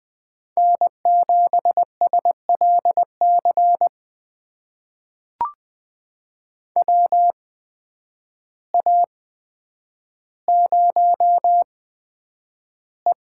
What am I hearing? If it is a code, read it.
Morse code, heard as a keyed tone, about 20 wpm, spaced as for 9 wpm, 700 Hz. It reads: N7SLC WA0E
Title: USA Call Signs - 8x Character Spacing - 20wpm